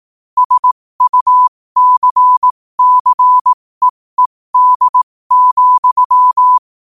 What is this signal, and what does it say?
Morse code, heard as a keyed tone, about 18 wpm, spaced as for 15 wpm, 1000 Hz. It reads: SUCCEED,
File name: qotdfiles / 15wpm